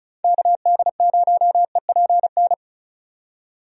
Morse code, heard as a keyed tone, about 36 wpm, 700 Hz. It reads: KD0EPN
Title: USA Call Signs - 6 Characters - 35wpm